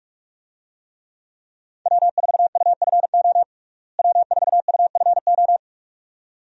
Morse code, heard as a keyed tone, about 45 wpm, 700 Hz. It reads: W4UFO W4UFO
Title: USA Call Signs - 5 Characters - Repeated 2x - 45wpm